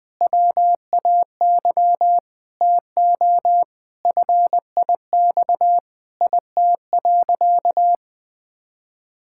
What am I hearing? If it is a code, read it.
Morse code, heard as a keyed tone, about 20 wpm, 700 Hz. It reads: WAY TO FIX IT.
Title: Sentences with 7-Word Phrases - 20wpm